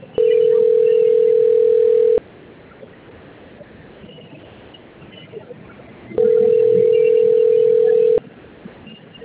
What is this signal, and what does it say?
Ringback tone (North America), 440 + 480 Hz.